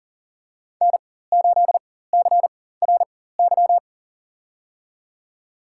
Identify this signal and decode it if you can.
Morse code, heard as a keyed tone, about 39 wpm, spaced as for 19 wpm, 700 Hz. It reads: N8CRY
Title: USA Call Signs - 5 Characters - 4x Character Spacing - 40wpm